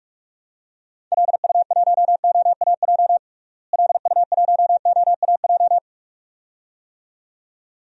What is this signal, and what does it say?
Morse code, heard as a keyed tone, about 44 wpm, 700 Hz. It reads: LU1OAJ LU1OAJ
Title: DXCC-100 Call Signs - 6 Characters - Repeated 2x - 45wpm